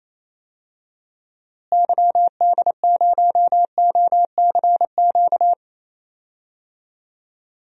Morse code, heard as a keyed tone, about 28 wpm, 700 Hz. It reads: YD0OCQ
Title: DXCC-100 Call Signs - 6 Characters - Repeated 3x - 28wpm